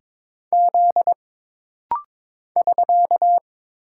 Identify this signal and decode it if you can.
Morse code, heard as a keyed tone, about 22 wpm, 700 Hz. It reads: Z <SK>
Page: This rapid-fire timing is geared to push your ICR (Instant Character Recognition) skills.